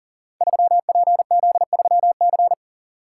Morse code, heard as a keyed tone, about 42 wpm, 690 Hz. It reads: 3PZ3C
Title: Code Groups - 5 Letter:Numbers - 40wpm